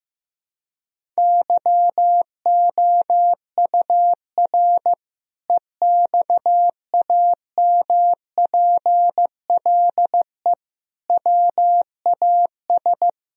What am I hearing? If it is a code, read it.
Morse code, heard as a keyed tone, about 16 wpm, 700 Hz. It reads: YOUR EXAMPLE WAS
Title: Sentences from Top 1000 Words - Review Part 1 - 15wpm